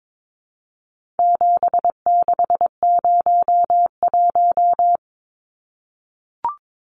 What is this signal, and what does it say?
Morse code, heard as a keyed tone, about 22 wpm, 700 Hz. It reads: 7601
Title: Numbers - 4-Digit 22wpm